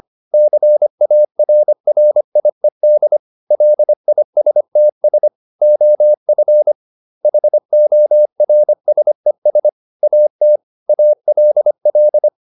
Morse code, heard as a keyed tone, about 25 wpm, 600 Hz. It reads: CARRIED LISTS OF HORSES AT ALL
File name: the old man and the sea part 1 25wpm